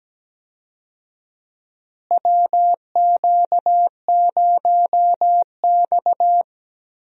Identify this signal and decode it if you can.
Morse code, heard as a keyed tone, about 17 wpm, 700 Hz. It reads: WQ0X
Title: USA Call Signs - 4 Characters - Repeated 3x - 17wpm